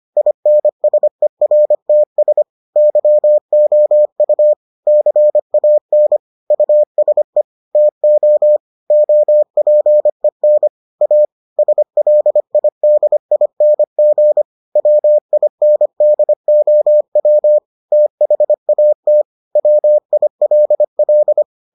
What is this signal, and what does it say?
Morse code, heard as a keyed tone, about 25 wpm, 600 Hz. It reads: INSERTS YOU CAN USE TO OPEN A SLIDING WINDOW THAT WILL